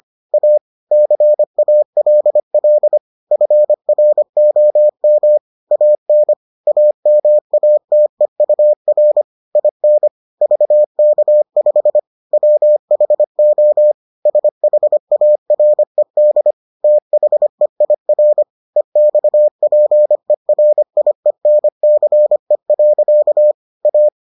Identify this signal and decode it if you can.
Morse code, heard as a keyed tone, about 25 wpm, 600 Hz. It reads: A CALL FROM AN AMATEUR IN VK5 WHO SHARED THEIR EXPERIENCE. A